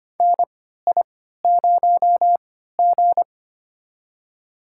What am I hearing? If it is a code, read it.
Morse code, heard as a keyed tone, about 25 wpm, spaced as for 14 wpm, 700 Hz. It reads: NI0G
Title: USA Call Signs - 3x Character Spacing - 25wpm